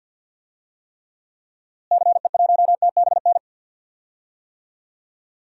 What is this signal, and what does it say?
Morse code, heard as a keyed tone, about 53 wpm, 700 Hz. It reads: CE1TBN
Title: DXCC-100 Call Signs - 6 Characters - Repeated 3x - 50wpm